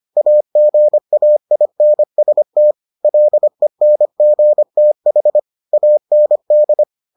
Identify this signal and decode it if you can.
Morse code, heard as a keyed tone, about 25 wpm, 600 Hz. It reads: AGAINST LENGTH AND